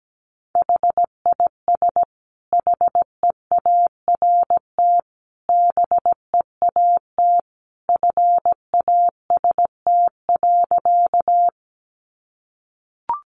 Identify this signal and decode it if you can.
Morse code, heard as a keyed tone, about 17 wpm, 700 Hz. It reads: HIS HEART BEAT FAST.
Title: Sentences from Top 1300 Words 17wpm